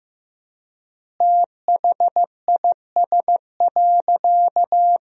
Morse code, heard as a keyed tone, about 15 wpm, 700 Hz. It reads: THIS.
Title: 3x Word-Spacing - Sentences from Top 400 Words - 15wpm